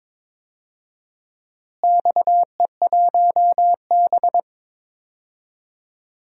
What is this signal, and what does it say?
Morse code, heard as a keyed tone, about 22 wpm, 700 Hz. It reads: XE1B
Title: DXCC-100 Call Signs - 4 Characters - Repeated 2x - 22wpm